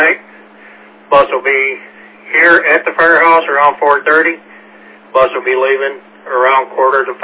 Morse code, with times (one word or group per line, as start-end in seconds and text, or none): none